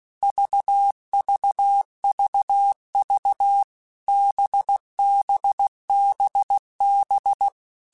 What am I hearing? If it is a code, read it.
Morse code, heard as a keyed tone, about 16 wpm, 800 Hz. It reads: VVVV BBBB
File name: Basic Letter Groups By7Groups / VB46 / VB46@15